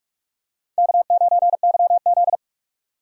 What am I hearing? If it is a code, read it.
Morse code, heard as a keyed tone, about 44 wpm, 700 Hz. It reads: K9YZ
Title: USA Call Signs - 4 Characters - 2x Character Spacing - 45wpm